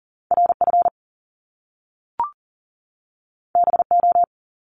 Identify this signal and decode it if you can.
Morse code, heard as a keyed tone, about 38 wpm, 700 Hz. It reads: RF BO
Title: Top Two Letter Combinations in Words 40wpm